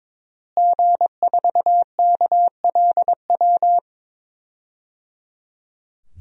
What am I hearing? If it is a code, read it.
Morse code, heard as a keyed tone, about 21 wpm, 700 Hz. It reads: G4KLW
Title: DXCC-100 Call Signs - 5 Characters - Repeated 3x - 22wpm